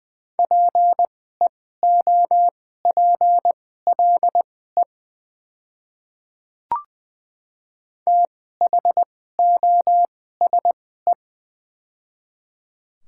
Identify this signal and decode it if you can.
Morse code, heard as a keyed tone, about 20 wpm, spaced as for 14 wpm, 700 Hz. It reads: PEOPLE THOSE
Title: 2x Character Spacing - Top 300 Words - 20wpm